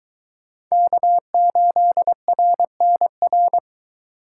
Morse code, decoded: K8RNR

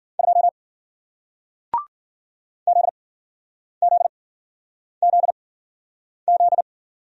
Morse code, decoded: 2 ZZZZ

